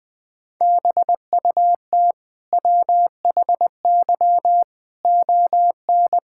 Morse code, heard as a keyed tone, about 20 wpm, 700 Hz. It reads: BUT WHY ON
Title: Sentences from Top 1000 Words - Review Part 6 - 20wpm